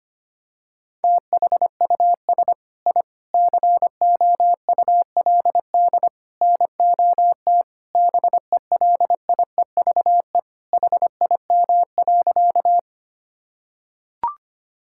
Morse code, heard as a keyed tone, about 25 wpm, 700 Hz. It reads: THUS I COULD NOT BELIEVE HIM.
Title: Sentences from Top 1000 Words - Review Part 5 - 25wpm